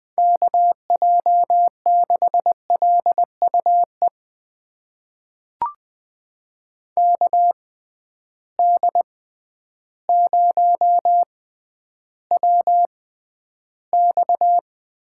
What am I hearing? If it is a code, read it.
Morse code, heard as a keyed tone, about 20 wpm, spaced as for 11 wpm, 700 Hz. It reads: KJ6LUE KD0WX
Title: USA Call Signs - 6 Characters - 6x Character Spacing - 20wpm